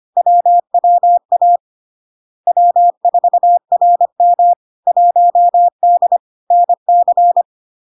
Morse code, heard as a keyed tone, about 25 wpm, 700 Hz. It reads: WWA W4RM 1D NC
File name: LARK-FD-2022-Log-in-CW-25wpm-700-Hz